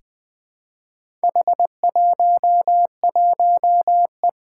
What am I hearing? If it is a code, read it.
Morse code, heard as a keyed tone, about 20 wpm, 700 Hz. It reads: H11E
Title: Four Challenging Letter-Number Combinations 20wpm